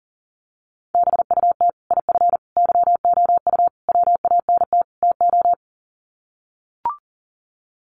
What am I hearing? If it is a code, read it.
Morse code, heard as a keyed tone, about 40 wpm, 700 Hz. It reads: BUT IF YOU WANT TO
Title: Sets of 5 Words Encore2 40wpm